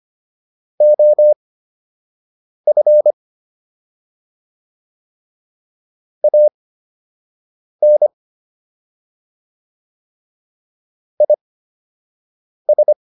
Morse code, heard as a keyed tone, about 25 wpm, spaced as for 6 wpm, 600 Hz. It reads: OF AN IS